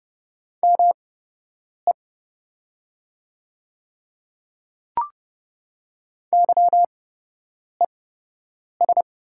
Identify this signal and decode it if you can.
Morse code, heard as a keyed tone, about 30 wpm, spaced as for 8 wpm, 690 Hz. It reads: ME YES
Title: 8x Character Spacing - Top 500 Words - 30wpm